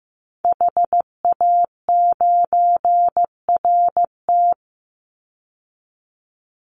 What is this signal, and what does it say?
Morse code, heard as a keyed tone, about 15 wpm, 700 Hz. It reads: HA9RT
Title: Call Signs Repeated 3x - 15wpm